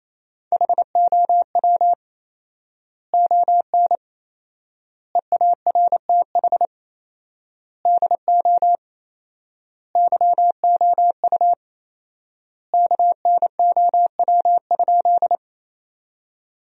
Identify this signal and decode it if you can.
Morse code, heard as a keyed tone, about 28 wpm, 700 Hz. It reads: HOW ON EARTH DO YOU KNOW?
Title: 4x Word-Spacing - Sentences from Top 200 Words - 28wpm